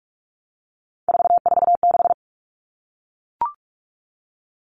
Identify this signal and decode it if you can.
Morse code, heard as a keyed tone, about 44 wpm, 700 Hz. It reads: 446
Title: Numbers - 3-Digit 45wpm